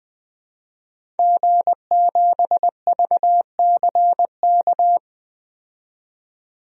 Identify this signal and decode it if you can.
Morse code, heard as a keyed tone, about 20 wpm, 700 Hz. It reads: G7VCK
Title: DXCC-100 Call Signs - 5 Characters - 20wpm